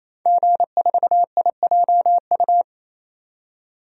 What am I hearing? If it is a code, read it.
Morse code, heard as a keyed tone, about 27 wpm, 700 Hz. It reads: G4IJU
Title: DXCC-100 Call Signs - 5 Characters - 28wpm